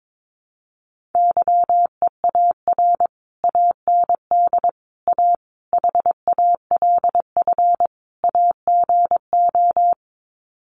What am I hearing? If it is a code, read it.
Morse code, heard as a keyed tone, about 22 wpm, 700 Hz. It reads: YEAR AND A HALF AGO
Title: Sets of 5 Words Encore 22wpm